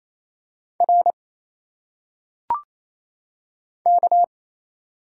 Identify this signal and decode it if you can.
Morse code, heard as a keyed tone, about 27 wpm, 700 Hz. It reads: R K
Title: Single Letters - Phonetic Pronunciation 28wpm